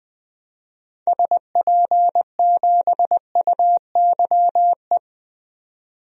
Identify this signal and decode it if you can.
Morse code, heard as a keyed tone, about 20 wpm, 700 Hz. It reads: SP7UYE
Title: DXCC-100 Call Signs - 6 Characters - 20wpm